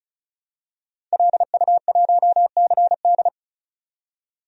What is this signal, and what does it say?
Morse code, heard as a keyed tone, about 34 wpm, 700 Hz. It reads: LU1CD
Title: DXCC-100 Call Signs - 5 Characters - 35wpm